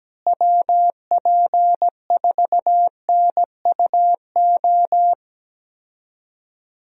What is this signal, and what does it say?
Morse code, heard as a keyed tone, about 17 wpm, 700 Hz. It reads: WP4NUO
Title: USA Call Signs - 6 Characters - Repeated 3x - 17wpm